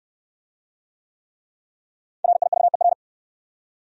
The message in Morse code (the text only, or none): RIVER